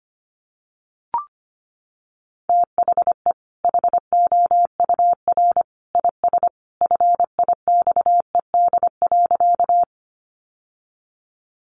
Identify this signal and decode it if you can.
Morse code, heard as a keyed tone, about 25 wpm, 700 Hz. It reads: THE HOUR IS FIXED.
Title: Sentences from Top 1100 Words 25wpm